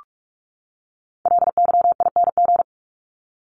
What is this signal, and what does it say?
Morse code, heard as a keyed tone, about 44 wpm, 700 Hz. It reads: LYING
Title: ING Suffix - Top 100 Words with Base Words 45wpm